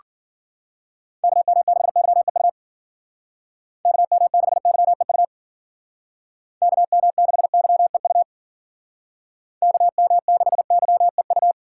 Morse code, heard as a keyed tone, about 49 wpm, 700 Hz. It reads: KM6YEU KM6YEU KM6YEU KM6YEU
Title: Speed Racing - USA Call Signs 40wpm